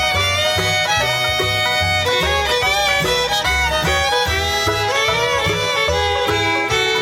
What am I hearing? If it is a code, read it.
Dial-up modem handshake.